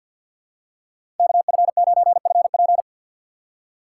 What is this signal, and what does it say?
Morse code, heard as a keyed tone, about 47 wpm, 700 Hz. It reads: KF9FP
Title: USA Call Signs - 5 Characters - 2x Character Spacing - 50wpm